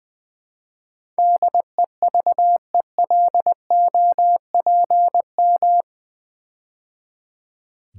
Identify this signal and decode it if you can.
Morse code, heard as a keyed tone, about 20 wpm, 700 Hz. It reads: DEVELOPM
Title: Build A Word - 4 Letters At A Time - 11-Letter Words - 20wpm